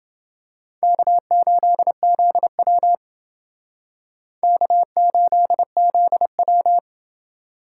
Morse code, heard as a keyed tone, about 29 wpm, 700 Hz. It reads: K8ZW K8ZW